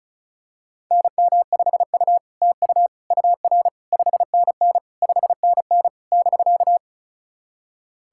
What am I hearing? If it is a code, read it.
Morse code, heard as a keyed tone, about 34 wpm, 700 Hz. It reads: NM5U TU UR 5NN 5NN <BK>
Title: SOTA QSO Practice 35wpm